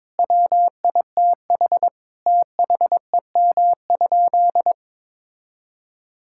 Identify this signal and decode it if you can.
Morse code, heard as a keyed tone, about 23 wpm, 700 Hz. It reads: WITH THEM?
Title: Sentences from Top 1000 Words - Review Part 5 - 22wpm